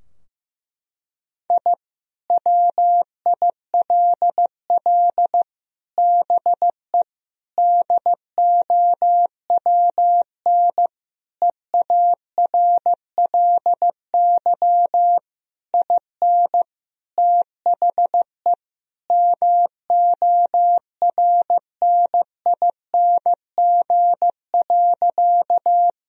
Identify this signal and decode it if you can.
Morse code, heard as a keyed tone, about 15 wpm, 700 Hz. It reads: I WILL BE DOWN EARLY IN THE MORNING.
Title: Sentences from Top 1000 Words - Review Part 2 - 15wpm